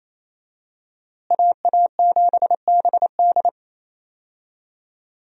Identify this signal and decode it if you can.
Morse code, heard as a keyed tone, about 28 wpm, 700 Hz. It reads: AA7BD
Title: USA Call Signs - 5 Characters - Repeated 2x - 28wpm